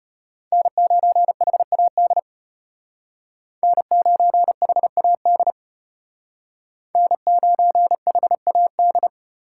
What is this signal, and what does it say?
Morse code, heard as a keyed tone, about 33 wpm, 700 Hz. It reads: N9HAD N9HAD N9HAD